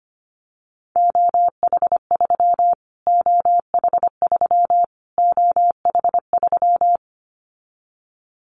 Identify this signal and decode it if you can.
Morse code, heard as a keyed tone, about 25 wpm, 700 Hz. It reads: OH3 OH3 OH3
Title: Callsign Prefixes for Top 25 DXCC - Repeated 3x - 25wpm